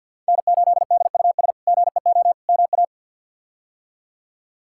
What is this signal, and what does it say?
Morse code, heard as a keyed tone, about 53 wpm, 690 Hz. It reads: N8DUS GEO MA
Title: K1USN SST QSO Practice - 50wpm